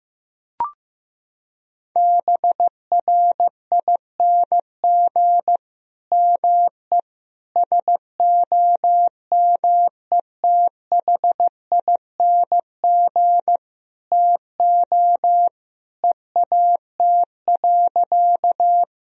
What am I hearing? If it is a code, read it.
Morse code, heard as a keyed tone, about 15 wpm, 700 Hz. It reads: BRING ME SOMETHING TO EAT.
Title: Sentences from Top 1000 Words - Review Part 3 - 15wpm